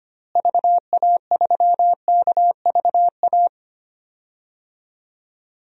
Morse code, heard as a keyed tone, about 25 wpm, 700 Hz. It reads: VA3KVA
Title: DXCC-100 Call Signs - 6 Characters - Repeated 2x - 25wpm